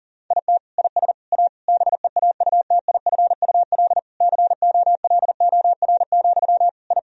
Morse code, heard as a keyed tone, about 40 wpm, 700 Hz. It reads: IT IS A BEAUTIFUL COLOR, I
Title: Sentences from Top 1000 Words - Review Part 2 - 40wpm